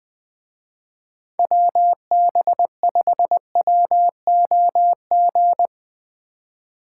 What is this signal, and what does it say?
Morse code, heard as a keyed tone, about 20 wpm, 700 Hz. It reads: WB5WOG